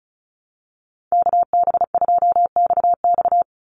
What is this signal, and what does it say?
Morse code, heard as a keyed tone, about 34 wpm, 700 Hz. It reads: KB2XX
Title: NA SOTA Chaser Callsigns 35wpm